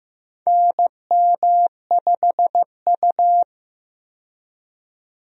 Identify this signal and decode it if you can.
Morse code, heard as a keyed tone, about 15 wpm, 700 Hz. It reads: NM5U
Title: USA Call Signs - 4 Characters - 15wpm